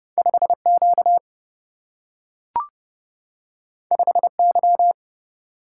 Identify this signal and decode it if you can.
Morse code, heard as a keyed tone, about 30 wpm, 700 Hz. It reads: 5 Q 5 Y
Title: Single-Letter Single-Number 30wpm